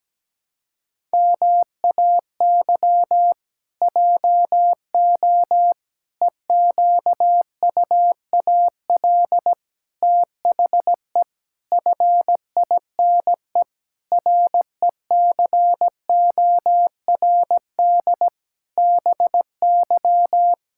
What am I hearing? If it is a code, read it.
Morse code, heard as a keyed tone, about 17 wpm, 700 Hz. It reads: MAY JO EQUAL THE FINE RECORD BY
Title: Pangram Challenge v2 - 17wpm